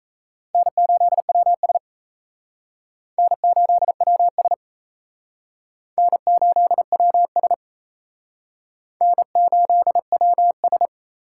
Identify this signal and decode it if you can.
Morse code, heard as a keyed tone, about 34 wpm, 700 Hz. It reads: N8WS N8WS N8WS N8WS
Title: Speed Racing - USA 1x2 Call Signs 28wpm